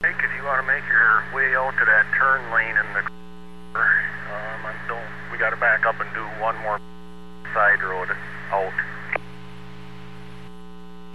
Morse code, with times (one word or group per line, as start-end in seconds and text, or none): none